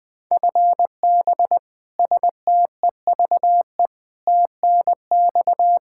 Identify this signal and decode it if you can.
Morse code, heard as a keyed tone, about 20 wpm, 700 Hz. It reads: FB STEVE TNX
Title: Practice QSO - Exchange 3 - More Info - 20wpm